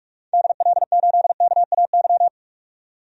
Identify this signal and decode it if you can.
Morse code, heard as a keyed tone, about 41 wpm, 700 Hz. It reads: DL8KAY
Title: DXCC-100 Call Signs - 6 Characters - 45wpm